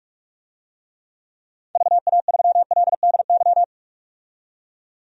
Morse code, heard as a keyed tone, about 46 wpm, 700 Hz. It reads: VA3LDY
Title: DXCC-100 Call Signs - 6 Characters - 45wpm